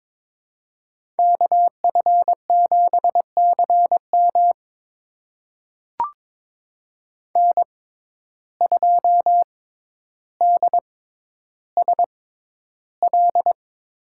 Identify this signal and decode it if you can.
Morse code, heard as a keyed tone, about 22 wpm, spaced as for 11 wpm, 700 Hz. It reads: KF7CM N2DSL
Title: USA Call Signs - 5 Characters - 6x Character Spacing - 22wpm